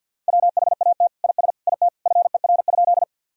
Morse code, heard as a keyed tone, about 50 wpm, 700 Hz. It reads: WHAT IS IT FER?
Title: Sentences from Top 100 Words and Abbreviations 50wpm